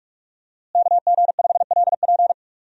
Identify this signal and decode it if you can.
Morse code, heard as a keyed tone, about 46 wpm, 700 Hz. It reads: KG5LP
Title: USA Call Signs - 5 Characters - Repeated 3x - 45wpm